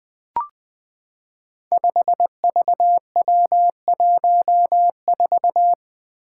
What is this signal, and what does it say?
Morse code, heard as a keyed tone, about 20 wpm, 700 Hz. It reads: 5VW14
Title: Five Challenging Letter-Number Combinations 20wpm